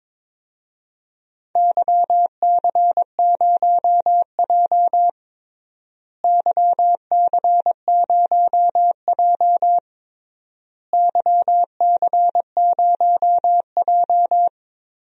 Morse code, heard as a keyed tone, about 22 wpm, 700 Hz. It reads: YC0J YC0J YC0J
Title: DXCC-100 Call Signs - 4 Characters - Repeated 3x - 22wpm